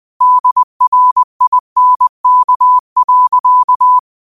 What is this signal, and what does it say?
Morse code, heard as a keyed tone, about 20 wpm, 1000 Hz. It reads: DRINK.
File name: qotdfiles / 20wpm